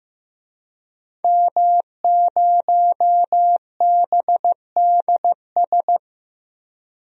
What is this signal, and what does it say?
Morse code, heard as a keyed tone, about 15 wpm, 700 Hz. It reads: M0BDS
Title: DXCC-100 Call Signs - 5 Characters - 15wpm